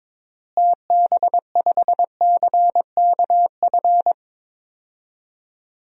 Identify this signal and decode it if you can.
Morse code, heard as a keyed tone, about 22 wpm, 700 Hz. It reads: TB5CKF
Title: DXCC-100 Call Signs - 6 Characters - Repeated 2x - 22wpm